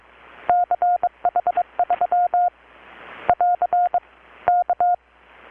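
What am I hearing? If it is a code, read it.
Morse code, heard as a keyed tone, about 23 wpm, 720 Hz. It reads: CH3 <AR> K
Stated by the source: Morse from KPH's Night of Nights XX maritime radiotelegraphy event, received in Chiapas, Mexico about 3,800 km away on a Kenwood R-600 with a grounded 10 m long-wire antenna.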